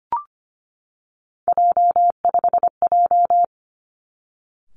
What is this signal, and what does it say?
Morse code, heard as a keyed tone, about 25 wpm, 700 Hz. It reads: J5J
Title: Three Challenging Letter-Number Combinations 25wpm